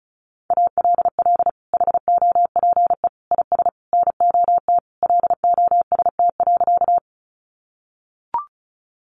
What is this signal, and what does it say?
Morse code, heard as a keyed tone, about 34 wpm, 700 Hz. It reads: ALL HOPE IS NOT LOST.
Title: Sentences from Top 1400 Words 35wpm